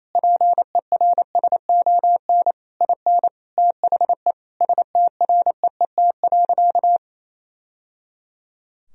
Morse code, heard as a keyed tone, about 28 wpm, 700 Hz. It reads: PERSON IN THE STREET.
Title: Sentences from Top 1000 Words - Review Part 3 - 28wpm